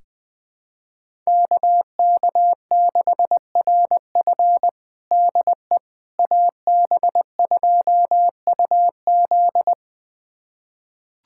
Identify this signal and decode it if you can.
Morse code, heard as a keyed tone, about 20 wpm, 700 Hz. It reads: KK6RF DE AB2UZ
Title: USA Callsign DE Callsign - 5 Characters - 20wpm